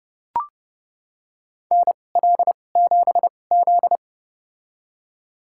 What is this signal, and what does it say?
Morse code, heard as a keyed tone, about 30 wpm, spaced as for 22 wpm, 700 Hz. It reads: NL7Z
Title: USA Call Signs - 2x Character Spacing - 30wpm